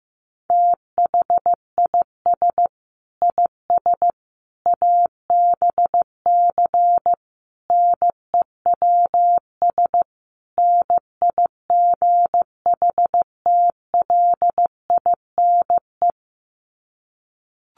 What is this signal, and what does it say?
Morse code, heard as a keyed tone, about 15 wpm, 700 Hz. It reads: THIS IS ABC NEWS NIGHTLINE
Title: Sets of 5 Words 15wpm